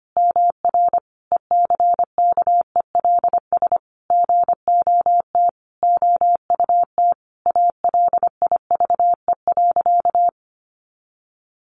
Morse code, heard as a keyed tone, about 25 wpm, 700 Hz. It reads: MR ECKELS GOT OUT ALIVE.
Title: A Sound Of Thunder 25wpm